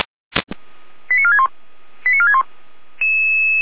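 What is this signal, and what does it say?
Five-tone selective call (ZVEI), heard as quick runs of five tones, tones of about 70 ms. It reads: 89461 89461
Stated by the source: ZVEI five-tone selective call received on an SDR.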